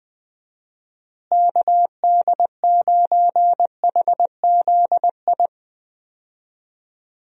Morse code, heard as a keyed tone, about 21 wpm, 700 Hz. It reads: KD9HZI
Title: USA Call Signs - 6 Characters - Repeated 2x - 20wpm